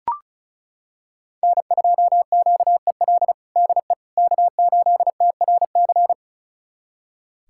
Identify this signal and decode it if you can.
Morse code, heard as a keyed tone, about 34 wpm, 700 Hz. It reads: N2QEL DE K8TRC
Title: USA Callsign DE Callsign - 5 Characters - 35wpm